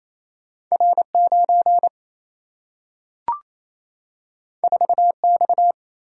Morse code, heard as a keyed tone, about 28 wpm, 700 Hz. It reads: R9 4X
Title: Single-Letter Single-Number 28wpm